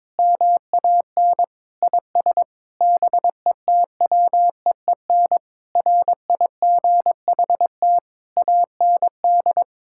Morse code, heard as a keyed tone, about 22 wpm, 700 Hz. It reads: MAN IS BETWEEN RIGHT AND
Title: Sentences from Top 1600 Words 22wpm